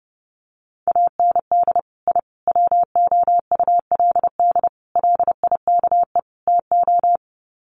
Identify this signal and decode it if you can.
Morse code, heard as a keyed tone, about 30 wpm, 700 Hz. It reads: AND I WOULD LIKE TO
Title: Sets of 5 Words Encore 30wpm